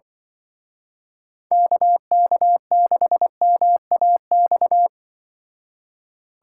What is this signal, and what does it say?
Morse code, heard as a keyed tone, about 23 wpm, 700 Hz. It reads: KK6MAX
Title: Speed Racing - USA Call Signs 20wpm